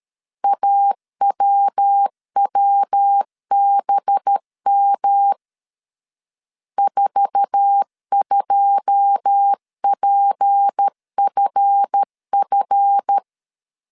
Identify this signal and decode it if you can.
Morse code, heard as a keyed tone, about 13 wpm, 800 Hz. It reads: AWWBM 42PFF
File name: CW / DET / 1 9 1500C S A 12wpm 3800